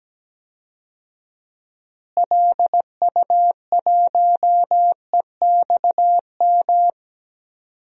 Morse code, heard as a keyed tone, about 17 wpm, 700 Hz. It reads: LU1EXM
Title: DXCC-100 Call Signs - 6 Characters - Repeated 3x - 17wpm